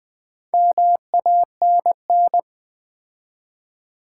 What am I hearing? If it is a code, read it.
Morse code, heard as a keyed tone, about 20 wpm, 700 Hz. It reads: MANN
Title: Build A Word - 2 Letters At A Time - 6-Letter Words - 20wpm